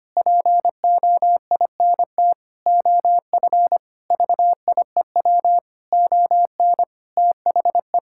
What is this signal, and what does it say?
Morse code, heard as a keyed tone, about 25 wpm, 700 Hz. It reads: POINT OF VIEW ON THE
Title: Sentences from Top 1000 Words - Review Part 6 - 25wpm